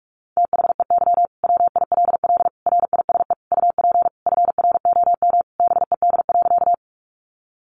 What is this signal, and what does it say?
Morse code, heard as a keyed tone, about 44 wpm, 700 Hz. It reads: THEY WILL RISE UP FROM BED.
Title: Sentences from Top 900 Words 45wpm